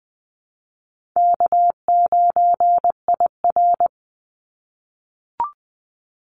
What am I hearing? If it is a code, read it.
Morse code, heard as a keyed tone, about 20 wpm, 700 Hz. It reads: K9IR
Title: Call Signs - 20wpm